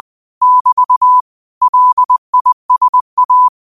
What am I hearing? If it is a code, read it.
Morse code, heard as a keyed tone, about 20 wpm, 1000 Hz. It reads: <BT> LISA
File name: qotdfiles / 20wpm